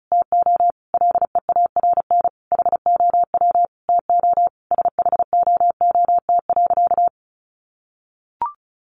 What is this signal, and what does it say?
Morse code, heard as a keyed tone, about 34 wpm, 700 Hz. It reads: TO LEARN HOW TO SHOOT.
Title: Sentences from Top 1200 Words 35wpm